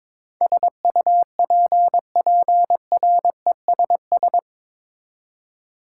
Morse code, heard as a keyed tone, about 22 wpm, 700 Hz. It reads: SUPPRESS
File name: practice / 22wpm / Eight Letter Words - 22wpm